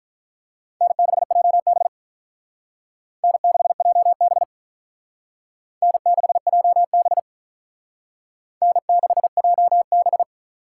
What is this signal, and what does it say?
Morse code, heard as a keyed tone, about 45 wpm, 700 Hz. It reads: N6JB N6JB N6JB N6JB